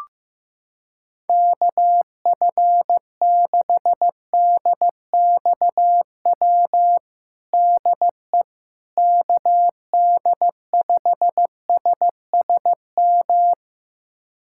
Morse code, heard as a keyed tone, about 15 wpm, 700 Hz. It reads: KF6DXW DE KD5SSM
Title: USA Callsign DE Callsign - 6 Characters - 15wpm